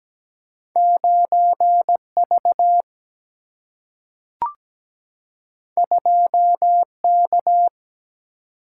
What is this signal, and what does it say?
Morse code, heard as a keyed tone, about 17 wpm, 700 Hz. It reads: 9V 2K
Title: Single-Letter Single-Number 17wpm